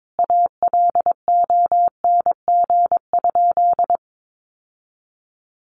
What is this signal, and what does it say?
Morse code, heard as a keyed tone, about 22 wpm, 700 Hz. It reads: ALONG?
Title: Sentences from Top 1300 Words 22wpm